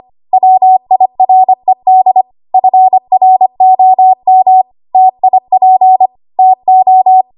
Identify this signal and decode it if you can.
Morse code, heard as a keyed tone, about 25 wpm, 750 Hz. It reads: WIRED FROM TIP TO
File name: Archive / 25 WPM / 25WPM